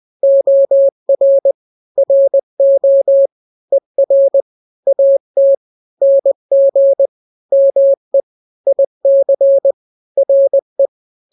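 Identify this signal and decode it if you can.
Morse code, heard as a keyed tone, about 20 wpm, 550 Hz. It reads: OR RO ER AT NG ME IC RE
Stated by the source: Morse drill of two-letter n-grams at 20/18 WPM (Farnsworth), from KK7RR.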